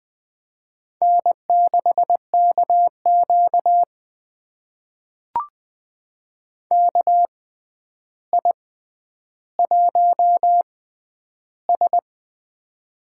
Morse code, heard as a keyed tone, about 20 wpm, spaced as for 10 wpm, 700 Hz. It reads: N6KQ KI1S
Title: USA Call Signs - 4 Characters - 6x Character Spacing - 20wpm